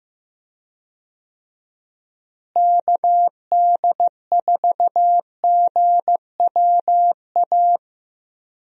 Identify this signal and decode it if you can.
Morse code, heard as a keyed tone, about 15 wpm, 700 Hz. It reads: KD4GWA
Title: USA Call Signs - 6 Characters - Repeated 2x - 15wpm